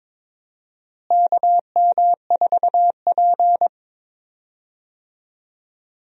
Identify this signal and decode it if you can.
Morse code, heard as a keyed tone, about 22 wpm, 700 Hz. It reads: KM4P
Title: USA Call Signs - 4 Characters - Repeated 3x - 22wpm